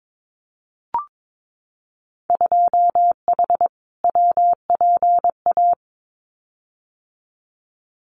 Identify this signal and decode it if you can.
Morse code, heard as a keyed tone, about 22 wpm, 700 Hz. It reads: 2H WPA
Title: Winter Field Day Exchanges 22wpm